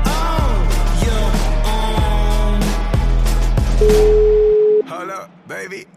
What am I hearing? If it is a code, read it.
Ringback tone (Europe, 425 Hz), with speech.